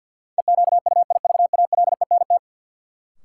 Morse code, heard as a keyed tone, about 53 wpm, 700 Hz. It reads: EQUIVALENT
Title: Sentences from Top 1600 Words - New Word Drill Alt - 50wpm